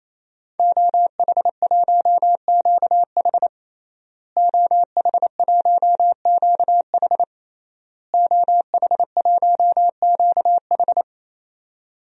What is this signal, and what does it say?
Morse code, heard as a keyed tone, about 28 wpm, 700 Hz. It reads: OH1QH OH1QH OH1QH
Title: DXCC-100 Call Signs - 5 Characters - Repeated 3x - 28wpm